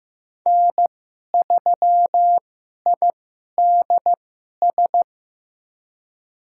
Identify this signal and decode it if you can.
Morse code, heard as a keyed tone, about 15 wpm, spaced as for 11 wpm, 700 Hz. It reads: N3IDS
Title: USA Call Signs - 5 Characters - 2x Character Spacing - 15wpm